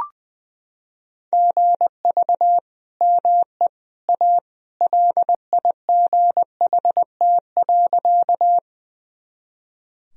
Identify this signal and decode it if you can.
Morse code, heard as a keyed tone, about 21 wpm, 700 Hz. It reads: GV ME A LIGHT.
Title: Sentences from Top 200 Words and Abbreviations 20wpm